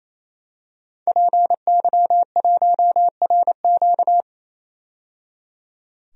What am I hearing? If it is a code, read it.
Morse code, heard as a keyed tone, about 28 wpm, 700 Hz. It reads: PY1RQ
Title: DXCC-100 Call Signs - 5 Characters - 28wpm